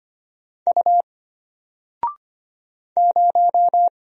Morse code, heard as a keyed tone, about 25 wpm, 700 Hz. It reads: U 0
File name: Single-Letter-Number-25wpm